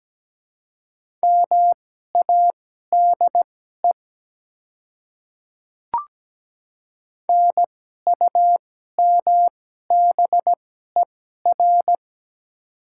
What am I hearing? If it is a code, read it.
Morse code, heard as a keyed tone, about 17 wpm, spaced as for 12 wpm, 700 Hz. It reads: MADE NUMBER
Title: 2x Character Spacing - Top 200 Words - 17wpm